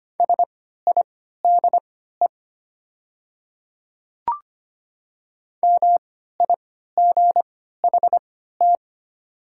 Morse code, heard as a keyed tone, about 25 wpm, spaced as for 14 wpm, 700 Hz. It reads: SIDE MIGHT
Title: 3x Character Spacing - Top 300 Words - 25wpm